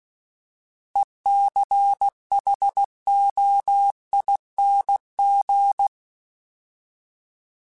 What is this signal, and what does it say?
Morse code, heard as a keyed tone, about 16 wpm, 800 Hz. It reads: ECHOING
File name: Alphabetical / Alphabetical mp3s / Es @15